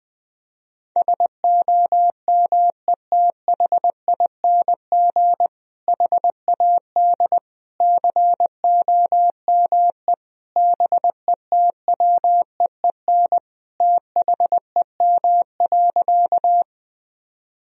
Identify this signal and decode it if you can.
Morse code, heard as a keyed tone, about 20 wpm, 700 Hz. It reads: SOMETHING HAD COME BETWEEN THEM.
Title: Sentences from Top 1000 Words - Review Part 1 - 20wpm